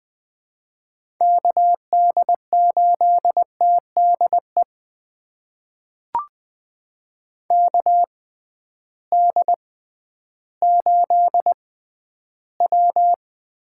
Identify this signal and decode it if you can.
Morse code, heard as a keyed tone, about 20 wpm, spaced as for 12 wpm, 700 Hz. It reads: KD8TDE KD8W
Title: USA Call Signs - 6 Characters - 6x Character Spacing - 20wpm